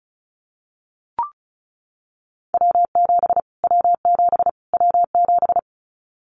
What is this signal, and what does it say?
Morse code, heard as a keyed tone, about 34 wpm, 700 Hz. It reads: W7 W7 W7
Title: Callsign Prefixes for Top 25 DXCC - Repeated 3x - 35wpm